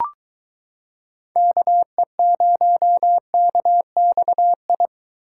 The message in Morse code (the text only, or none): KE0KXI